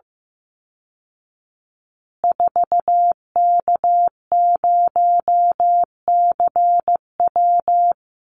Morse code, heard as a keyed tone, about 15 wpm, 700 Hz. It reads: 4K0CW
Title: Call Signs Repeated 3x - 15wpm